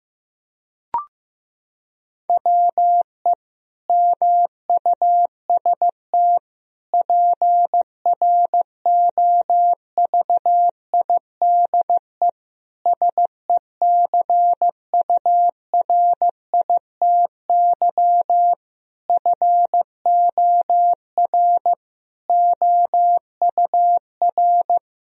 Morse code, heard as a keyed tone, about 15 wpm, 700 Hz. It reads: WE MUST PROVIDE SECURITY FOR OUR
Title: Sentences from Top 1000 Words - Review Part 4 - 15wpm